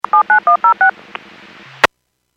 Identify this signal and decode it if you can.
DTMF, heard as telephone touch tones, tones of about 110 ms. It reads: 79106
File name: Businesses / 00345 03-06 Businesses 451.25 WPS